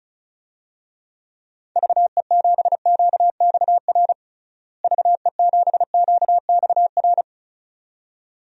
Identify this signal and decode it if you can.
Morse code, heard as a keyed tone, about 36 wpm, 700 Hz. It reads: VE7QXR VE7QXR
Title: DXCC-100 Call Signs - 6 Characters - Repeated 2x - 35wpm